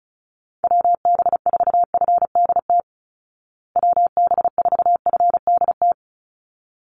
Morse code, heard as a keyed tone, about 34 wpm, 700 Hz. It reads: WB4FDT WB4FDT
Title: Call Signs Repeated 3x - 35wpm